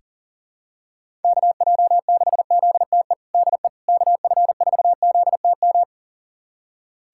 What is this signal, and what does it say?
Morse code, heard as a keyed tone, about 40 wpm, 700 Hz. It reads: KJ6ZTE DE KF4ZTM
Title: USA Callsign DE Callsign - 6 Characters - 40wpm